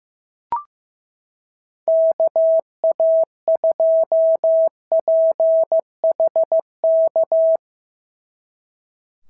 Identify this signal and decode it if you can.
Morse code, heard as a keyed tone, about 15 wpm, 690 Hz. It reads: KA2PHK
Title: SOTA QSO Practice 15wpm